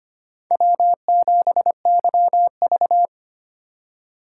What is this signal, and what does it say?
Morse code, heard as a keyed tone, about 25 wpm, 700 Hz. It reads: W7YV
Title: USA Call Signs - 4 Characters - 4x Character Spacing - 25wpm